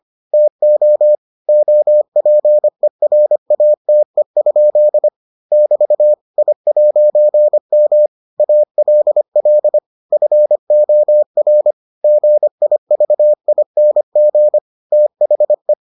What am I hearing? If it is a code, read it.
Morse code, heard as a keyed tone, about 25 wpm, 600 Hz. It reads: TO OPERATE? <BT> I'M ALL FOR GIVING THE